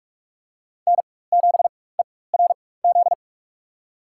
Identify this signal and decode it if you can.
Morse code, heard as a keyed tone, about 46 wpm, spaced as for 21 wpm, 700 Hz. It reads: N7ERZ
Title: USA Call Signs - 5 Characters - 4x Character Spacing - 45wpm